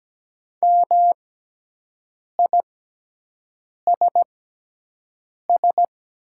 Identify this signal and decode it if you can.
Morse code, heard as a keyed tone, about 17 wpm, spaced as for 6 wpm, 700 Hz. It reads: MISS
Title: 6x Character Spacing - Top 500 Words - 17wpm